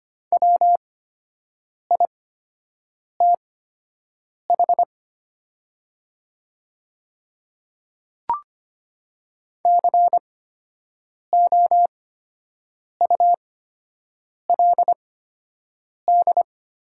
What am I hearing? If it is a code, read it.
Morse code, heard as a keyed tone, about 25 wpm, spaced as for 7 wpm, 700 Hz. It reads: WITH COULD
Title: 8x Character Spacing - Top 100 Words - 25wpm